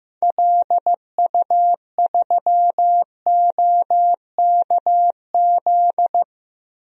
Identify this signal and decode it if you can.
Morse code, heard as a keyed tone, about 15 wpm, 700 Hz. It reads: LU3OKZ
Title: DXCC-100 Call Signs - 6 Characters - 15wpm